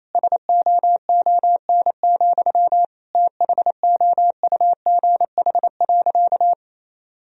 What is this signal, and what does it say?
Morse code, heard as a keyed tone, about 28 wpm, 700 Hz. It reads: SOON, THOUGH.
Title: Sentences from Top 1000 Words - Review Part 2 - 28wpm